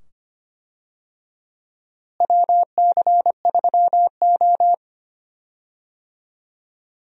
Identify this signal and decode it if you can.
Morse code, heard as a keyed tone, about 25 wpm, 700 Hz. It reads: WC3O
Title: USA Call Signs - 4 Characters - Repeated 2x - 25wpm